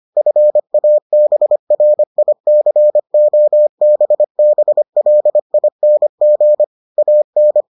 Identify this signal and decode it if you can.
Morse code, heard as a keyed tone, about 25 wpm, 600 Hz. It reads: FABRICOBBLING AN